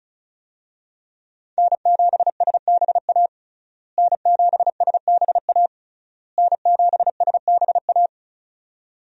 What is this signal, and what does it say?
Morse code, heard as a keyed tone, about 36 wpm, 700 Hz. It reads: N7SBA N7SBA N7SBA
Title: USA Call Signs - 5 Characters - Repeated 3x - 35wpm